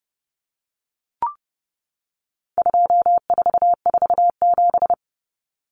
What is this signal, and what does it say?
Morse code, heard as a keyed tone, about 30 wpm, 700 Hz. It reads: 2447
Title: Numbers - 4-Digit 30wpm